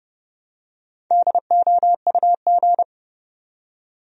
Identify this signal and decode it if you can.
Morse code, heard as a keyed tone, about 30 wpm, 700 Hz. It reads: DOUG
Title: USA Common Names - 30wpm